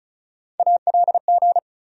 Morse code, heard as a keyed tone, about 36 wpm, 700 Hz. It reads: ALG
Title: USA Call Sign Postfixes - 3 Letters 35wpm